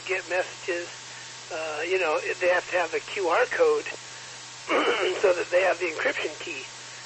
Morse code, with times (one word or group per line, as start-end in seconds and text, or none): none